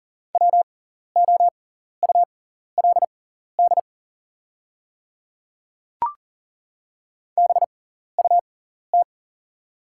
Morse code, heard as a keyed tone, about 40 wpm, spaced as for 14 wpm, 690 Hz. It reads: WOULD BUT
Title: 6x Character Spacing - Top 100 Words - 40wpm